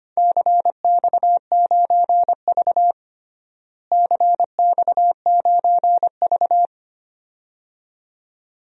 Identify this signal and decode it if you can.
Morse code, heard as a keyed tone, about 25 wpm, 700 Hz. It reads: CX9V CX9V
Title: DXCC-100 Call Signs - 4 Characters - Repeated 2x - 25wpm